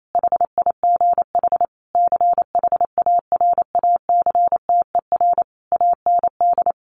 Morse code, heard as a keyed tone, about 28 wpm, 700 Hz. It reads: HIGH CHARACTER AND
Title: Sentences from Top 1200 Words 28wpm